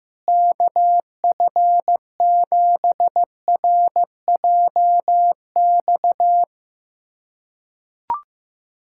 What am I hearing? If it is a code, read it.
Morse code, heard as a keyed tone, about 15 wpm, 700 Hz. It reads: KF7RJX
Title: USA Callsign DE Callsign - 6 Characters - 15wpm